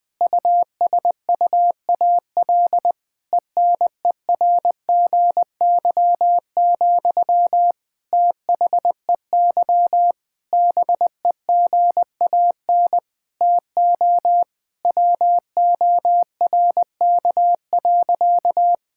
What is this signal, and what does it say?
Morse code, heard as a keyed tone, about 19 wpm, 700 Hz. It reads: USUAL ENERGY, THEY BEGAN TO WORK.
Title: Sentences from Top 1000 Words - Review Part 6 - 20wpm